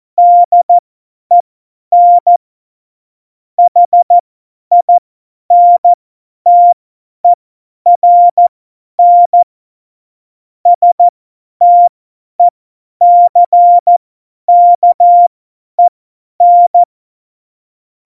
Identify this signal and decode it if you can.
Morse code, heard as a keyed tone, about 14 wpm, spaced as for 10 wpm, 700 Hz. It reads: DEN HINTERN STECKEN
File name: Morsekurs / Witze / Witz 13. Oktober 2020, 50 F70